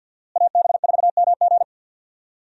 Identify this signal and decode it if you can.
Morse code, heard as a keyed tone, about 53 wpm, 700 Hz. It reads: AB4MG
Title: USA Call Signs - 5 Characters - Repeated 2x - 50wpm